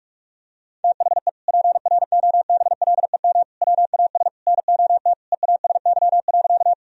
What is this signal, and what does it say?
Morse code, heard as a keyed tone, about 42 wpm, 700 Hz. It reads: THE PROBLEM WAS NOT EASY.
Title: Sentences from Top 1000 Words - Review Part 2 - 45wpm